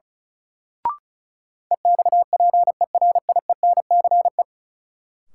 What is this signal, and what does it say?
Morse code, heard as a keyed tone, about 36 wpm, 700 Hz. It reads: EXPERIENCE
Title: Sentences from Top 600 Words - New Word Drill - 35wpm